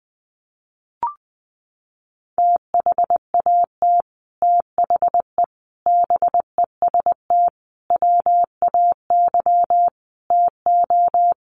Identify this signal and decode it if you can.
Morse code, heard as a keyed tone, about 20 wpm, 700 Hz. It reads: THAT THE BEST WAY TO
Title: Sets of 5 Words Encore2 20wpm